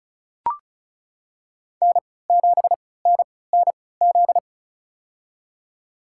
Morse code, heard as a keyed tone, about 36 wpm, spaced as for 20 wpm, 700 Hz. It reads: N7NNZ
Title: USA Call Signs - 3x Character Spacing - 35wpm